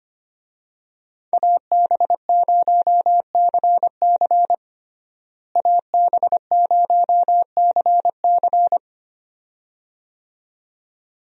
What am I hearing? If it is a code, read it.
Morse code, heard as a keyed tone, about 25 wpm, 700 Hz. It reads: AB0CC AB0CC
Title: USA Call Signs - 5 Characters - Repeated 2x - 25wpm